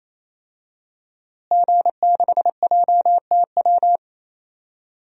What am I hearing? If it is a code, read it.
Morse code, heard as a keyed tone, about 28 wpm, 700 Hz. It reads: G6JTW
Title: DXCC-100 Call Signs - 5 Characters - Repeated 3x - 28wpm